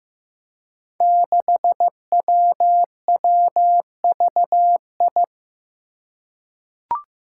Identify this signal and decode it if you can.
Morse code, heard as a keyed tone, about 15 wpm, 700 Hz. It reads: 6WWVI
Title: Five Challenging Letter-Number Combinations 15wpm